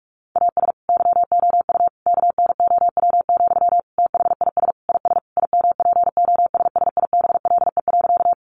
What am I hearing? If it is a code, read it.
Morse code, heard as a keyed tone, about 44 wpm, 700 Hz. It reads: AS YOU KNOW, THIS IS IMPOSSIBLE.
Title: Sentences from Top 1100 Words 45wpm